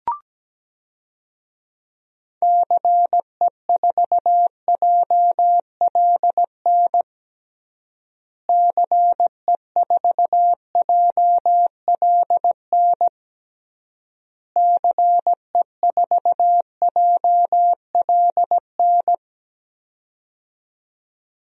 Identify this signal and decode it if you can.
Morse code, heard as a keyed tone, about 17 wpm, 700 Hz. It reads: CE4JLN CE4JLN CE4JLN
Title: DXCC-100 Call Signs - 6 Characters - Repeated 3x - 17wpm